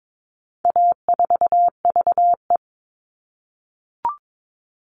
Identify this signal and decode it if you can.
Morse code, heard as a keyed tone, about 22 wpm, 700 Hz. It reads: A4VE
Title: All Cut Numbers - 4-Digit 22wpm